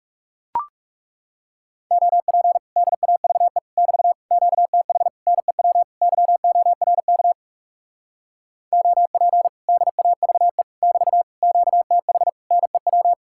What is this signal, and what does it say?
Morse code, heard as a keyed tone, about 40 wpm, 700 Hz. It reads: OP DAVE <BT> QTH NEW YORK OP DAVE <BT> QTH NEW